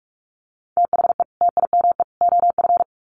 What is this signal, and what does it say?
Morse code, heard as a keyed tone, about 44 wpm, 700 Hz. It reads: THE TIME OF
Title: Sets of 3 Words 45wpm